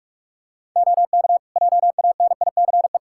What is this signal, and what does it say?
Morse code, heard as a keyed tone, about 47 wpm, 700 Hz. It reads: OK JANICE
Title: Practice QSO - Exchange 3 - More Info - 45wpm